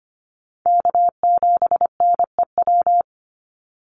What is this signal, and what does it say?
Morse code, heard as a keyed tone, about 25 wpm, 700 Hz. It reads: K7NEW
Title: NA SOTA Chaser Callsigns 25wpm